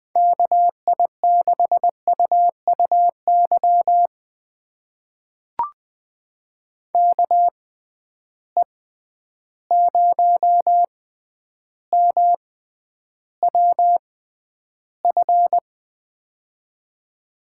Morse code, heard as a keyed tone, about 20 wpm, spaced as for 10 wpm, 700 Hz. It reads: KI6UUY KE0MWF
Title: USA Call Signs - 6 Characters - 6x Character Spacing - 20wpm